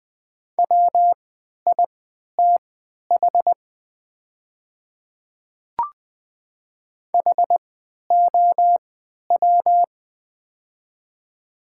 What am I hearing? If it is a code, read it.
Morse code, heard as a keyed tone, about 20 wpm, spaced as for 11 wpm, 700 Hz. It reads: WITH HOW
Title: 3x Character Spacing - Top 100 Words - 20wpm